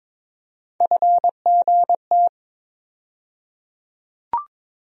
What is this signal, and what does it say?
Morse code, heard as a keyed tone, about 21 wpm, 700 Hz. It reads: FGT
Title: USA Call Sign Postfixes - 3 Letters 22wpm